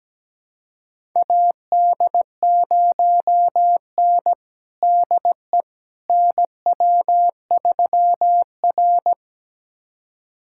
Morse code, heard as a keyed tone, about 17 wpm, 700 Hz. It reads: AD0N DE NW3R
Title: USA Callsign DE Callsign - 4 Characters - 17wpm